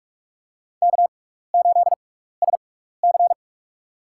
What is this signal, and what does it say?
Morse code, heard as a keyed tone, about 47 wpm, spaced as for 16 wpm, 700 Hz. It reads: K8SC
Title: USA Call Signs - 6x Character Spacing - 45wpm